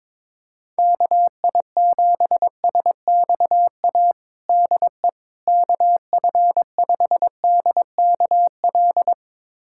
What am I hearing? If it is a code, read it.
Morse code, heard as a keyed tone, about 22 wpm, 700 Hz. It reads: KI7SXA DE KF5DKL
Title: USA Callsign DE Callsign - 6 Characters - 22wpm